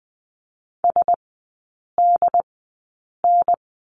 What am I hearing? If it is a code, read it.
Morse code, heard as a keyed tone, about 20 wpm, spaced as for 8 wpm, 690 Hz. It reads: SDN
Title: 081 Lesson 14e - Number 2 - Three Characters 20wpm